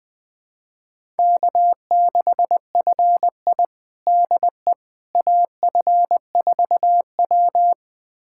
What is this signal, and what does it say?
Morse code, heard as a keyed tone, about 20 wpm, 700 Hz. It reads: K6FI DE AF4W
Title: USA Callsign DE Callsign - 4 Characters - 20wpm